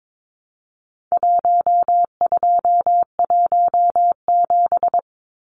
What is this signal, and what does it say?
Morse code, heard as a keyed tone, about 22 wpm, 700 Hz. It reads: 1217
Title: Numbers - Contest Style - 4-Digit 22wpm